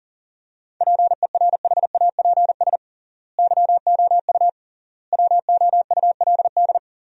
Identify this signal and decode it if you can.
Morse code, heard as a keyed tone, about 40 wpm, 700 Hz. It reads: PERHAPS YOU WOULD
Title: 3x Word-Spacing - Sentences from Top 500 Words - 40wpm